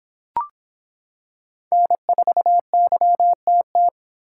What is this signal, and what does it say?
Morse code, heard as a keyed tone, about 24 wpm, 700 Hz. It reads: N4YTT